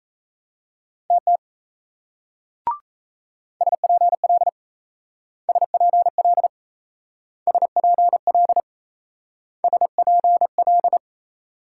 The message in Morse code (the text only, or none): TT SPL SPL SPL SPL